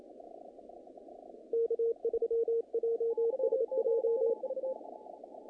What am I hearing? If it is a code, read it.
Morse code, heard as a keyed tone, about 27 wpm, 460 Hz, with radio noise. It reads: K3JSJ
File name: contests / naqp-cw-k5tr-op-k5pi- -2100